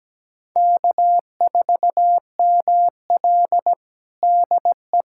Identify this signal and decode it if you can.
Morse code, heard as a keyed tone, about 17 wpm, 700 Hz. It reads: K4ML DE
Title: USA Callsign DE Callsign - 4 Characters - 17wpm